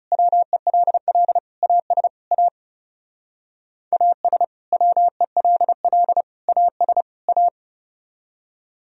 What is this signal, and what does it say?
Morse code, heard as a keyed tone, about 38 wpm, 700 Hz. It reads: WELL AS A AS WELL AS A